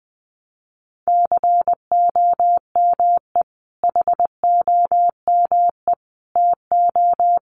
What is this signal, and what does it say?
Morse code, heard as a keyed tone, about 20 wpm, 700 Hz. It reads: COME HOME TO
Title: Sentences from Top 1000 Words 20wpm